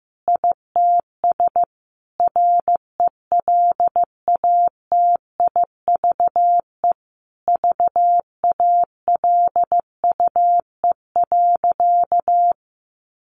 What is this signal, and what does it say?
Morse code, heard as a keyed tone, about 15 wpm, 700 Hz. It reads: ITS RELATIVE VALUE.
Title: Sentences from Top 1400 Words 15wpm